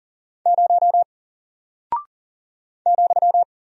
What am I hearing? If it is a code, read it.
Morse code, heard as a keyed tone, about 40 wpm, 700 Hz. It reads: 0 ,
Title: Single Character (Letter:Number:Punctuation:Prosign) Alt 40wpm